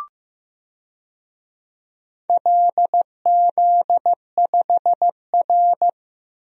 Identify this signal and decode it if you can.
Morse code, heard as a keyed tone, about 15 wpm, 700 Hz. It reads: LZ5R
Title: DXCC-100 Call Signs - 4 Characters - 15wpm